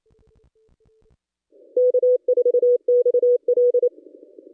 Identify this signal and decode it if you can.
Morse code, heard as a keyed tone, about 28 wpm, 490 Hz. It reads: K4XL
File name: contests / ss-cw-k5tr-op-n5rz -1100